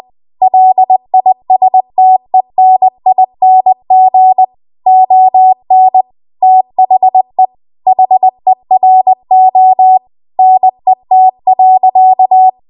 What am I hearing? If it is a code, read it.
Morse code, heard as a keyed tone, about 20 wpm, 750 Hz. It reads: LISTENING ON THE HERO NET.
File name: Archive / 20 WPM / 20WPM